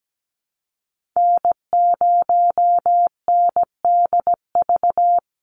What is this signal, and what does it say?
Morse code, heard as a keyed tone, about 17 wpm, 700 Hz. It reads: N0NDV
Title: USA Call Signs 17wpm